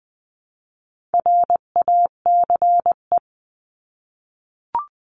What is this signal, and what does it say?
Morse code, heard as a keyed tone, about 20 wpm, 700 Hz. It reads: RACE
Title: 259 Lesson 42d - Words 18-20wpm